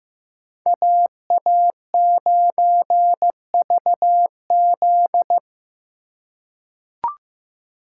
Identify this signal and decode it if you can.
Morse code, heard as a keyed tone, about 15 wpm, 700 Hz. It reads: AA9VZ
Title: USA Call Signs - 2x Character Spacing - 15wpm